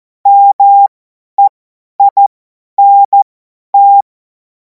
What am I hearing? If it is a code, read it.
Morse code, heard as a keyed tone, about 14 wpm, spaced as for 10 wpm, 800 Hz. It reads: MEINT
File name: Morsekurs / Witze / Witz 14. Juli 2026 50 F70